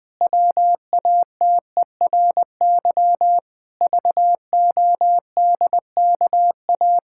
Morse code, heard as a keyed tone, about 20 wpm, 700 Hz. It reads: WATERY VODKA
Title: Pangram Challenge v2 - 20wpm